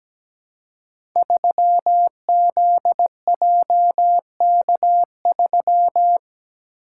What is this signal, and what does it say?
Morse code, heard as a keyed tone, about 17 wpm, 700 Hz. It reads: 3ZJK3
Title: Code Groups - 5 Letter:Numbers - 17wpm